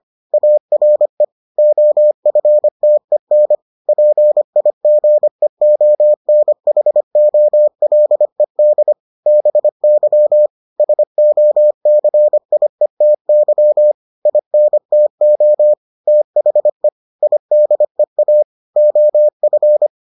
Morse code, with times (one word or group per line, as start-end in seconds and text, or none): 0.3-1.2 ARE
1.6-3.5 OFTEN
3.9-8.9 PIGEONHOLED
9.3-10.5 BY
10.8-13.9 SOCIETY
14.3-15.7 INTO
16.1-16.9 THE
17.2-18.4 IDEA
18.8-19.9 OF